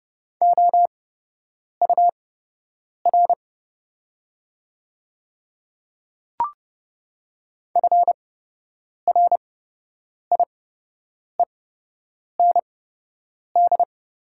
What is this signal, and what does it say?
Morse code, heard as a keyed tone, about 30 wpm, spaced as for 8 wpm, 700 Hz. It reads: OUR FRIEND
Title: 8x Character Spacing - Top 300 Words - 30wpm